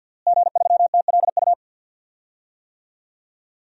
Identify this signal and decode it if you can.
Morse code, heard as a keyed tone, about 49 wpm, 700 Hz. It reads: G3TLU
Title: DXCC-100 Call Signs - 5 Characters - Repeated 3x - 50wpm